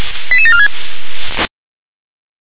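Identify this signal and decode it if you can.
Five-tone selective call (ZVEI), heard as quick runs of five tones, tones of about 70 ms. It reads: 88746
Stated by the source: ZVEI five-tone selective call received on an SDR.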